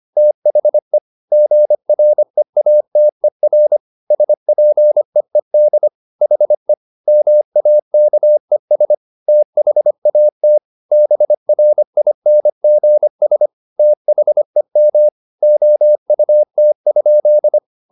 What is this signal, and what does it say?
Morse code, heard as a keyed tone, about 25 wpm, 600 Hz. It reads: THE GREATER SPEED HE MAKES THAT BRINGS THEM OUT?